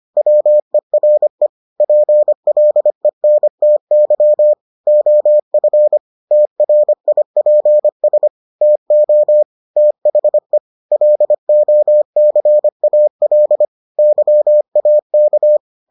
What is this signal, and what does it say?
Morse code, heard as a keyed tone, about 25 wpm, 600 Hz. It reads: WERE PLENTY OF TRIPS TO THE LOCAL YAK